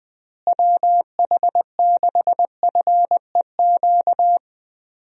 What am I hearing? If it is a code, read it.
Morse code, heard as a keyed tone, about 21 wpm, 700 Hz. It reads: WH6FEQ
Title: USA Call Signs - 6 Characters - Repeated 2x - 20wpm